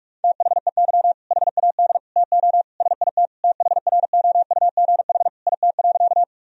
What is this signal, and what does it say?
Morse code, heard as a keyed tone, about 41 wpm, 700 Hz. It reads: THEY HAD TO SIT THROUGH IT.
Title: Sentences from Top 1000 Words - Review Part 1 - 45wpm